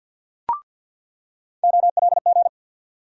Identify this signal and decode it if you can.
Morse code, heard as a keyed tone, about 47 wpm, 700 Hz. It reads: OLG
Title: USA Call Sign Postfixes - 3 Letters 50wpm